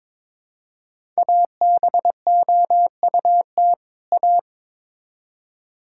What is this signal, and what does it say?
Morse code, heard as a keyed tone, about 22 wpm, 700 Hz. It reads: ABOUT A
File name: Build A Phrase - 2-Word Phrases - 22wpm